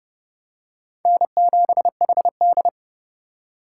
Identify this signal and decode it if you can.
Morse code, heard as a keyed tone, about 29 wpm, 700 Hz. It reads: N7HD